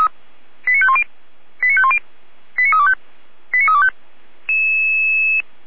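Five-tone selective call (ZVEI), heard as quick runs of five tones, tones of about 70 ms. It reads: 89510 89510 89325 89325
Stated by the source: ZVEI five-tone selective call received by SDR.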